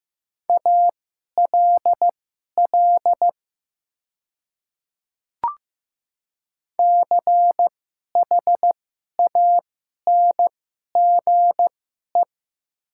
Morse code, heard as a keyed tone, about 15 wpm, spaced as for 11 wpm, 700 Hz. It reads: ALL CHANGE
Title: 2x Character Spacing - Top 200 Words - 15wpm